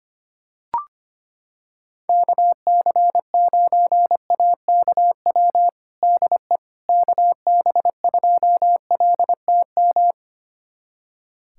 Morse code, heard as a keyed tone, about 25 wpm, 700 Hz. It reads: KC9AKW DE KB2LTM
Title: USA Callsign DE Callsign 25wpm